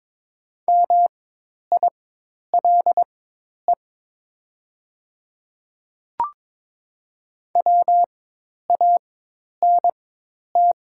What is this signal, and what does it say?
Morse code, heard as a keyed tone, about 23 wpm, spaced as for 10 wpm, 700 Hz. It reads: MILE WANT
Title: 4x Character Spacing - Top 500 Words - 22wpm